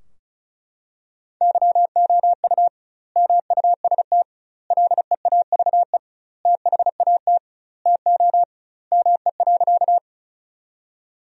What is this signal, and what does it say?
Morse code, heard as a keyed tone, about 36 wpm, 700 Hz. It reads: YOU MUST LEAVE THAT TO ME.
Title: 2x Word-Spacing - Sentences from Top 400 Words - 35wpm